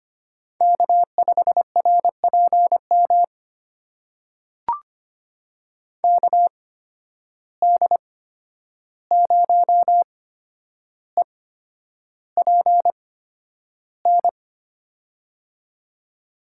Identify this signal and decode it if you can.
Morse code, heard as a keyed tone, about 25 wpm, spaced as for 10 wpm, 700 Hz. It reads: K5RPM KD0EPN
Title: USA Call Signs - 8x Character Spacing - 25wpm